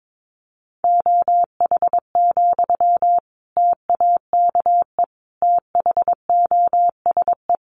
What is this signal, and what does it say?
Morse code, heard as a keyed tone, about 22 wpm, 700 Hz. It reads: OH, TAKE THOSE
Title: Sentences from Top 500 Words 22wpm